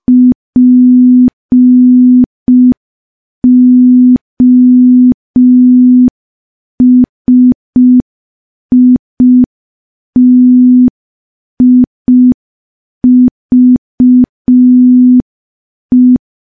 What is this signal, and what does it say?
Morse code, heard as a keyed tone, about 5 wpm, 250 Hz. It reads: POSITIVE
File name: morse / morse-training-05WPM